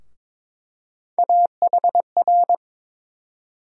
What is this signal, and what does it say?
Morse code, heard as a keyed tone, about 22 wpm, 700 Hz. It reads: AHR
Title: USA Call Sign Postfixes - 3 Letters 22wpm